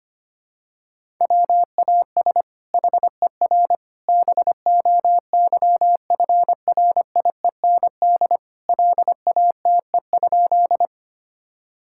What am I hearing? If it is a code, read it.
Morse code, heard as a keyed tone, about 25 wpm, 700 Hz. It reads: WAS HER BOYFRIEND LATE?
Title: Sentences from Top 1000 Words - Review Part 5 - 25wpm